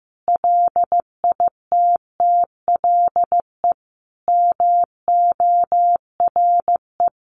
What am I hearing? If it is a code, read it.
Morse code, heard as a keyed tone, about 15 wpm, 700 Hz. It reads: LITTLE MORE
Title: Sentences from Top 1300 Words 15wpm